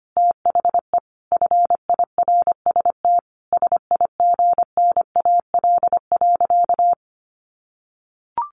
Morse code, heard as a keyed tone, about 25 wpm, 700 Hz. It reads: THE FIRST SIGNAL.
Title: Sentences from Top 1400 Words 25wpm